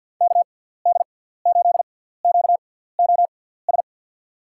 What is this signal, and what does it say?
Morse code, heard as a keyed tone, about 52 wpm, spaced as for 17 wpm, 700 Hz. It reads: KD8QOS
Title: USA Call Signs - 6 Characters - 6x Character Spacing - 50wpm